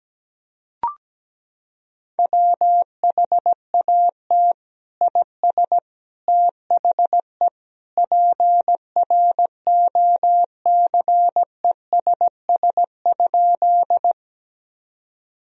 Morse code, heard as a keyed tone, about 17 wpm, 700 Hz. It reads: WHAT IS THE PROCESS?
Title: Sentences from Top 1000 Words - Review Part 3 - 17wpm